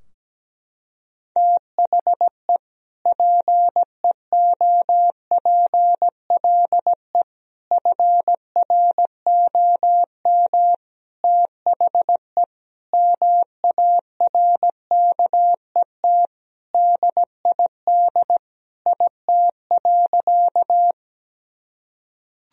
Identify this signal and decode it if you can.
Morse code, heard as a keyed tone, about 17 wpm, 700 Hz. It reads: THE PEOPLE FROM THE MARKET DID IT.
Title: Sentences from Top 1000 Words - Review Part 3 - 17wpm